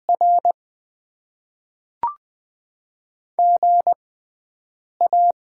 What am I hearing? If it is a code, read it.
Morse code, heard as a keyed tone, about 20 wpm, spaced as for 7 wpm, 690 Hz. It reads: R GA
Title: Top 20 QSO Elements - 6x Character Spacing - 20wpm